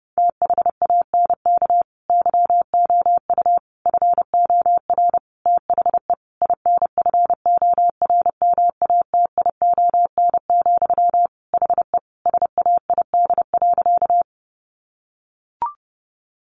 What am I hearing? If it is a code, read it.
Morse code, heard as a keyed tone, about 28 wpm, 700 Hz. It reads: THANK YOU FOR THE INFORMATION, HE SAID.
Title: Sentences from Top 600 Words 30wpm